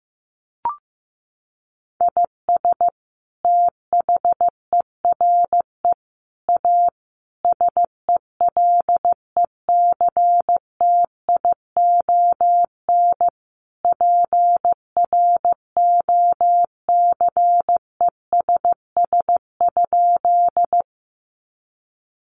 Morse code, heard as a keyed tone, about 15 wpm, 700 Hz. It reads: IS THERE A SELECTION PROCESS?
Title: Sentences from Top 1400 Words 15wpm